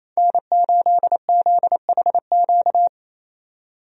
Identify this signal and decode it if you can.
Morse code, heard as a keyed tone, about 28 wpm, 700 Hz. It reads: N8ZHQ
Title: USA Call Signs - 5 Characters - 28wpm